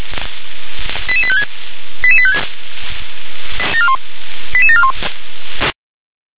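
Five-tone selective call (ZVEI), heard as quick runs of five tones, tones of about 70 ms. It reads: 88856 88856 88741
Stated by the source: Received by SDR as ZVEI five-tone selective call 88707.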